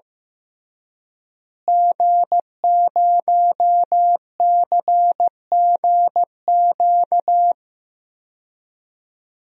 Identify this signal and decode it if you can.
Morse code, heard as a keyed tone, about 15 wpm, 700 Hz. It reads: G0CGQ
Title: DXCC-100 Call Signs - 5 Characters - Repeated 3x - 15wpm